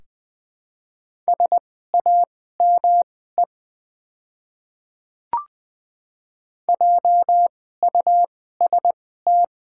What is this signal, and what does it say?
Morse code, heard as a keyed tone, about 20 wpm, spaced as for 14 wpm, 700 Hz. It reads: SAME JUST
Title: 2x Character Spacing - Top 200 Words - 20wpm